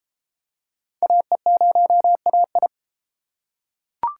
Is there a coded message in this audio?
Yes: Morse code from 1.0 to 2.7 s, heard as a keyed tone, about 32 wpm, 700 Hz. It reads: AE0AI